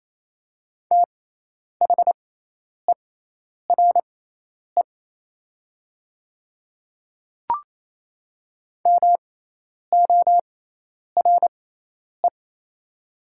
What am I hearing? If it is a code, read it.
Morse code, heard as a keyed tone, about 28 wpm, spaced as for 10 wpm, 700 Hz. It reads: THERE MORE
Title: 6x Character Spacing - Top 100 Words - 28wpm